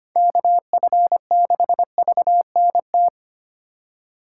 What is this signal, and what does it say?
Morse code, heard as a keyed tone, about 30 wpm, 700 Hz. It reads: KF6VNT
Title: Speed Racing - USA 2x3 Call Signs 25wpm